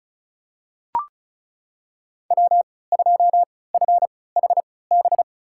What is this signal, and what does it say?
Morse code, heard as a keyed tone, about 34 wpm, spaced as for 20 wpm, 700 Hz. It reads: W2FHB
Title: USA Call Signs - 5 Characters - 3x Character Spacing - 35wpm